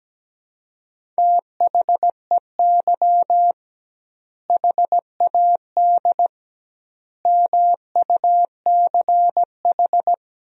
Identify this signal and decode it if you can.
Morse code, heard as a keyed tone, about 17 wpm, 700 Hz. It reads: THEY HAD MUCH
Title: 2x Word-Spacing - Sentences from Top 400 Words - 17wpm